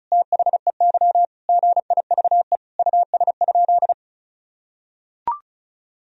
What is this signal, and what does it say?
Morse code, heard as a keyed tone, about 34 wpm, 700 Hz. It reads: THEY GIVE US?
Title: Sentences from Top 1000 Words - Review Part 2 - 35wpm